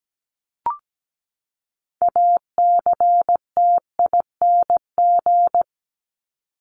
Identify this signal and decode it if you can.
Morse code, heard as a keyed tone, about 17 wpm, 700 Hz. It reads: ACTING
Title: ING Suffix - Top 100 Words with Base Words 17wpm